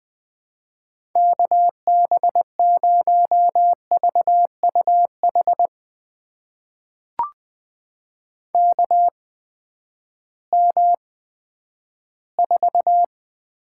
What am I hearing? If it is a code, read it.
Morse code, heard as a keyed tone, about 20 wpm, spaced as for 11 wpm, 700 Hz. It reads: KB0VUH KM4
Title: USA Call Signs - 6 Characters - 8x Character Spacing - 20wpm